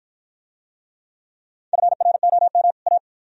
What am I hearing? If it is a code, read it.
Morse code, heard as a keyed tone, about 53 wpm, 700 Hz. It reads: FROM A